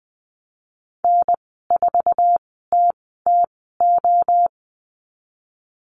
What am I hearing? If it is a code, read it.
Morse code, heard as a keyed tone, about 20 wpm, spaced as for 15 wpm, 700 Hz. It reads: N4TTO
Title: 088 Lesson 15g - Number 5 - Call Signs 20wpm